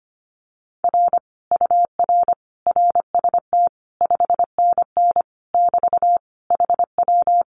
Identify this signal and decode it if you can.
Morse code, heard as a keyed tone, about 25 wpm, 700 Hz. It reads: R UR RST 5NN <BT> HW
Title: Common QSO Phrases Encore 25wpm